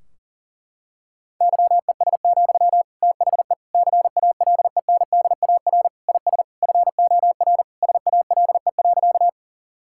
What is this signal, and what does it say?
Morse code, heard as a keyed tone, about 40 wpm, 700 Hz. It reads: YES, THE CALENDAR IS FOR SALE.
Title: Sentences from Top 1600 Words 40wpm